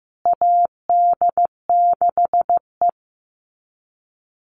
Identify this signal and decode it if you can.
Morse code, heard as a keyed tone, about 15 wpm, 700 Hz. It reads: AD6E
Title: Call Signs Repeated 2x - 15wpm